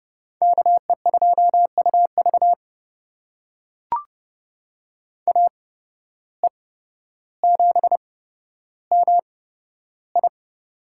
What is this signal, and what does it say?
Morse code, heard as a keyed tone, about 29 wpm, spaced as for 13 wpm, 700 Hz. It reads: KE2UV AE7MI
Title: USA Call Signs - 5 Characters - 8x Character Spacing - 30wpm